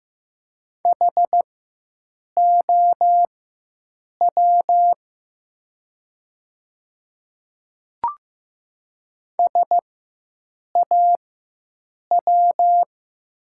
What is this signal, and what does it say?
Morse code, heard as a keyed tone, about 15 wpm, spaced as for 7 wpm, 700 Hz. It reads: HOW SAW
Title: 4x Character Spacing - Top 500 Words - 15wpm